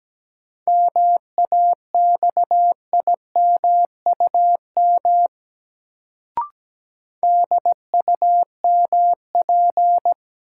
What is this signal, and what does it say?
Morse code, heard as a keyed tone, about 17 wpm, 700 Hz. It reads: MAXIMUM DUMP
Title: Sentences from Top 1300 Words - New Word Drill Alt - 17wpm